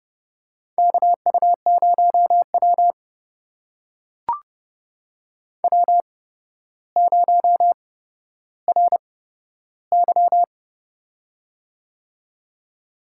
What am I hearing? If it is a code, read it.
Morse code, heard as a keyed tone, about 30 wpm, spaced as for 13 wpm, 700 Hz. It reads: KU0W W0RY
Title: USA Call Signs - 4 Characters - 8x Character Spacing - 30wpm